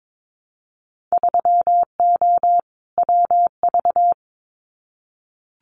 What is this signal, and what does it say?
Morse code, heard as a keyed tone, about 22 wpm, 700 Hz. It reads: 3O WV
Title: Winter Field Day Exchanges 22wpm